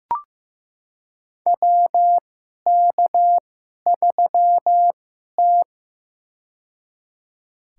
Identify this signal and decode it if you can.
Morse code, heard as a keyed tone, about 15 wpm, spaced as for 11 wpm, 700 Hz. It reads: WK3T
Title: USA Call Signs - 4 Characters - 2x Character Spacing - 15wpm